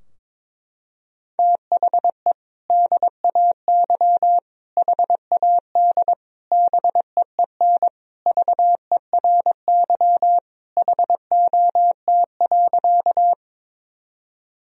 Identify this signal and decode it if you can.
Morse code, heard as a keyed tone, about 22 wpm, 700 Hz. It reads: THE DAY HAD BEEN VERY HOT.
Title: Sentences from Top 1000 Words - Review Part 3 - 22wpm